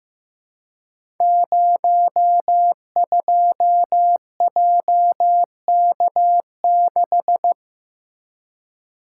Morse code, heard as a keyed tone, about 15 wpm, 700 Hz. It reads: 02JK6
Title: Code Groups - 5 Letter:Numbers - 15wpm